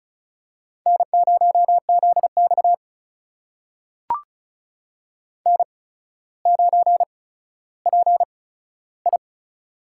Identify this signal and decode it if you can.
Morse code, heard as a keyed tone, about 36 wpm, spaced as for 15 wpm, 700 Hz. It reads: N0ZX N9PI
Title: USA Call Signs - 4 Characters - 8x Character Spacing - 35wpm